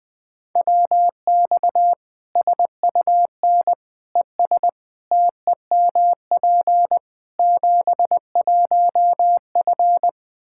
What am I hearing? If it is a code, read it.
Morse code, heard as a keyed tone, about 21 wpm, 700 Hz. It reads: WX SUN ES TEMP 71F
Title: Practice QSO - Exchange 2 - Info - 20wpm